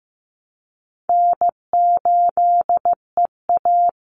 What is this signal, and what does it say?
Morse code, heard as a keyed tone, about 15 wpm, 700 Hz. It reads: N8EA
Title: Call Signs Repeated 3x - 15wpm